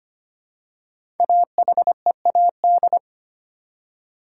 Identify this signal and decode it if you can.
Morse code, heard as a keyed tone, about 25 wpm, 700 Hz. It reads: AHEAD
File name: 5 Letter Words - First 3 Letters Given - 25wpm